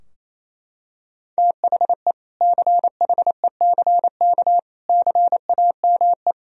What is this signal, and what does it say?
Morse code, heard as a keyed tone, about 28 wpm, 700 Hz. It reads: THE CHECK CAME
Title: Sentences from Top 1000 Words - Review Part 3 - 28wpm